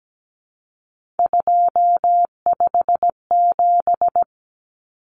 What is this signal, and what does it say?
Morse code, heard as a keyed tone, about 17 wpm, 700 Hz. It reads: 257
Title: Common Cut Numbers - 3-Digit 17wpm